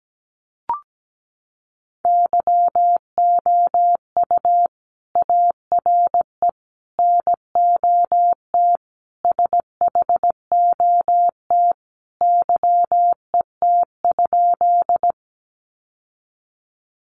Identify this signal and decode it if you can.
Morse code, heard as a keyed tone, about 17 wpm, 700 Hz. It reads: YOU ARE NOT SHOT YET?
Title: Sentences from Top 1100 Words 17wpm